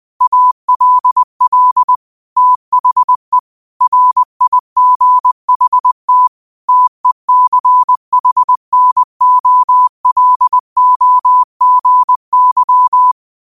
Morse code, heard as a keyed tone, about 20 wpm, 1000 Hz. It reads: ALL THE RIGHT TECHNOLOGY
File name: qotdfiles / 20wpm